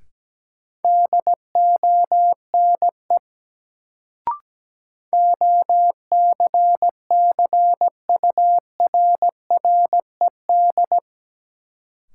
Morse code, heard as a keyed tone, about 17 wpm, 700 Hz. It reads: DONE OCCURRED
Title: Sets of 5 Words Encore - New Word Drill - 17wpm